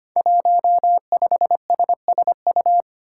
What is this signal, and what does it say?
Morse code, heard as a keyed tone, about 25 wpm, 700 Hz. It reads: 15SSU
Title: Five Challenging Letter-Number Combinations 25wpm